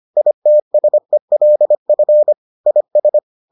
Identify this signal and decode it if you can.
Morse code, heard as a keyed tone, about 25 wpm, 600 Hz. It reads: ITSELF IS